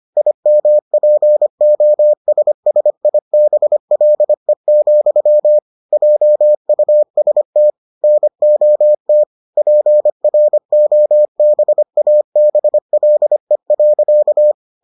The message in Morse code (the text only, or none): IMPOSSIBLE, JUST NOT PROBABLE.